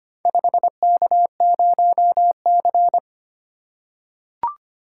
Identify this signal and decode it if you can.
Morse code, heard as a keyed tone, about 25 wpm, 700 Hz. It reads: 5K0C
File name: practice / 25wpm / Four Letter-Number Combinations - 25wpm